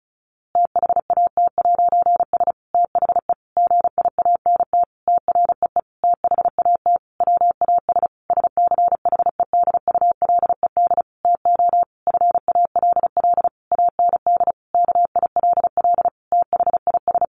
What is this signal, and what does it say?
Morse code, heard as a keyed tone, about 34 wpm, 700 Hz. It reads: THAT'S THE GIANT TREE THAT WAS SCHEDULED TO FALL AND KILL THIS
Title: A Sound Of Thunder 35wpm